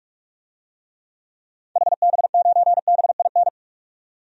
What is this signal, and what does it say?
Morse code, heard as a keyed tone, about 46 wpm, 700 Hz. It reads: HB9BIN
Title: DXCC-100 Call Signs - 6 Characters - 45wpm